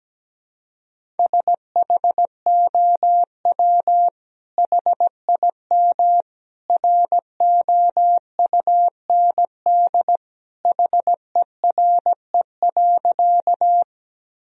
Morse code, heard as a keyed tone, about 17 wpm, 700 Hz. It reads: SHOW HIM ROUND HERE.
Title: Sentences from Top 1000 Words - Review Part 1 - 17wpm